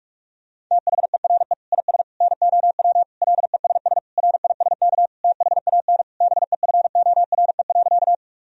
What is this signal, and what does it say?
Morse code, heard as a keyed tone, about 41 wpm, 700 Hz. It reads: THERE IS NOW LESS RISK THAN BEFORE.
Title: Sentences from Top 1000 Words - Review Part 4 - 45wpm